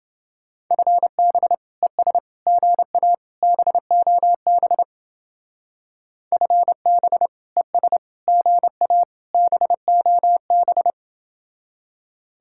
Morse code, heard as a keyed tone, about 30 wpm, 700 Hz. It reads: FB ES GA BOB FB ES GA BOB